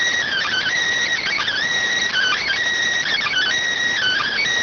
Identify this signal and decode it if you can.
Emergency Alert System (SAME) tones, heard as harsh buzzing bursts. It is heard from 0.3 to 4.3 s.